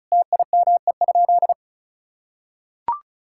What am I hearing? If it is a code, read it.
Morse code, heard as a keyed tone, about 36 wpm, 700 Hz. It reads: TIME?
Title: Sentences from Top 1000 Words - Review Part 5 - 35wpm